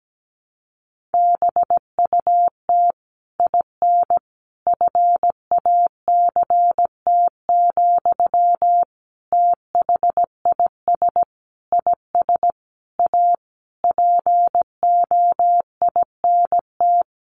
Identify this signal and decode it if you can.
Morse code, heard as a keyed tone, about 17 wpm, 700 Hz. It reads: BUT IN FACT, THIS IS A POINT
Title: Sentences from Top 1500 Words 17wpm